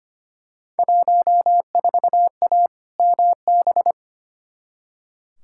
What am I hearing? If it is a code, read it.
Morse code, heard as a keyed tone, about 25 wpm, 700 Hz. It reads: 14A MB
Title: Field Day Exchanges Encore - 25wpm